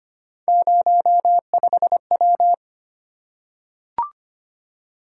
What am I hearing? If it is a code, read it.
Morse code, heard as a keyed tone, about 25 wpm, 700 Hz. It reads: 05W
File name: Three Letter-Number Combinations - 25wpm